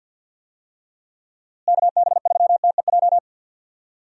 Morse code, heard as a keyed tone, about 50 wpm, 700 Hz. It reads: KB3TEJ
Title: USA Call Signs - 6 Characters - Repeated 2x - 50wpm